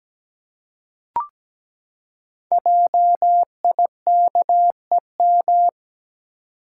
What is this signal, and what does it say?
Morse code, heard as a keyed tone, about 17 wpm, 700 Hz. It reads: JIKEM
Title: Code Groups - 5 Letters - 17wpm